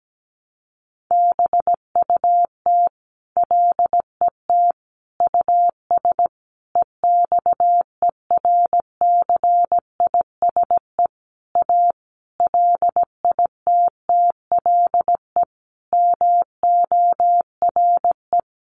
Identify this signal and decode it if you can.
Morse code, heard as a keyed tone, about 17 wpm, 700 Hz. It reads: BUT LET US EXERCISE A LITTLE MORE
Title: Sentences from Top 1300 Words 17wpm